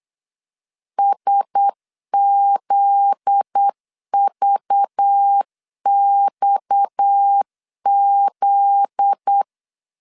Morse code, heard as a keyed tone, about 8 wpm, 800 Hz. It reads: SZVXZ
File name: LEZIONI / 7 GF9Z / 7 GF9Z 8wpm 3213